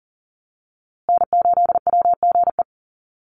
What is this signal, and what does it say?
Morse code, heard as a keyed tone, about 39 wpm, 700 Hz. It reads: N8WGE
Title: USA Call Signs Repeated 3x 40wpm